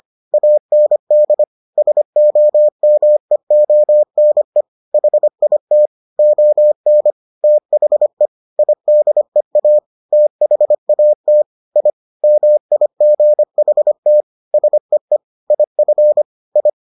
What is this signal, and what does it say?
Morse code, heard as a keyed tone, about 25 wpm, 600 Hz. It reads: AND SOMEONE HIT ON THE IDEA THAT I MIGHT SEE IF I